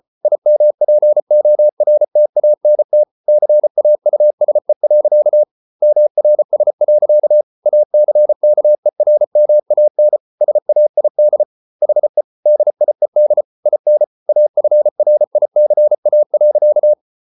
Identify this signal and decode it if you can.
Morse code, heard as a keyed tone, about 34 wpm, 600 Hz. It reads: IMPORTANT CAUSE. MRS. ACKERMAN SAID HE DIED IN AFRICA.